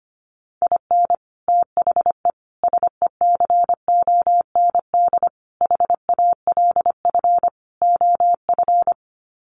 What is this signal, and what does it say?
Morse code, heard as a keyed tone, about 25 wpm, 700 Hz. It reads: IN THE SECOND HALF OF
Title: Sets of 5 Words 25wpm